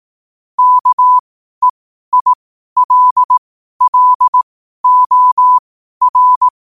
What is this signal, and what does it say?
Morse code, heard as a keyed tone, about 18 wpm, spaced as for 12 wpm, 1000 Hz. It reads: KEILLOR
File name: qotdfiles / 12wpm